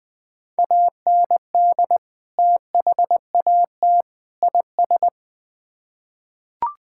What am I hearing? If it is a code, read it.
Morse code, heard as a keyed tone, about 20 wpm, 700 Hz. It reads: AND THAT IS
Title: Build A Phrase - 3-Word Phrases - 20wpm